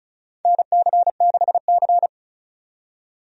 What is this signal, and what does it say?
Morse code, heard as a keyed tone, about 34 wpm, 700 Hz. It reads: NC6C
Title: USA Call Signs - 4 Characters - 35wpm